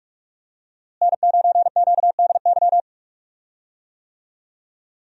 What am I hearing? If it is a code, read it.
Morse code, heard as a keyed tone, about 42 wpm, 700 Hz. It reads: N9QDY
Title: USA Call Signs - 5 Characters - Repeated 3x - 45wpm